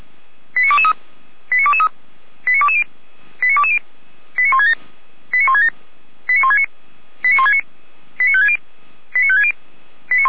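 Five-tone selective call (ZVEI), heard as quick runs of five tones, tones of about 80 ms. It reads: 89223 89223 89220 89220 89167 89167 89169 89169 89560 89560
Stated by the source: ZVEI five-tone selective call received on SDR.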